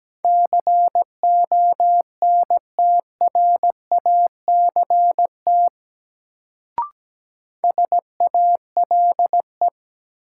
Morse code, heard as a keyed tone, about 17 wpm, 700 Hz. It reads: CONTRACT SALE
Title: Sentences from Top 1400 Words - New Word Drill - 17wpm